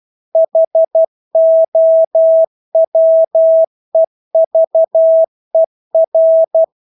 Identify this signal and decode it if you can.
Morse code, heard as a keyed tone, about 12 wpm, 650 Hz. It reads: HOWEVER
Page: Engelske ord 12 wpm